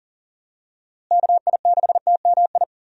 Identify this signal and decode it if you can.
Morse code, heard as a keyed tone, about 40 wpm, 700 Hz. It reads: KI6TMI
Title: USA Call Signs - 6 Characters - Repeated 3x - 40wpm